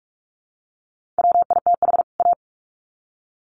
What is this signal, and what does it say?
Morse code, heard as a keyed tone, about 44 wpm, 700 Hz. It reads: WITH A
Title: Sets of 2 Words 45wpm